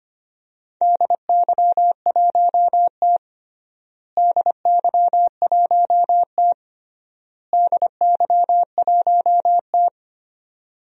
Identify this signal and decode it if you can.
Morse code, heard as a keyed tone, about 25 wpm, 700 Hz. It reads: DY1T DY1T DY1T
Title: DXCC-100 Call Signs - 4 Characters - Repeated 3x - 25wpm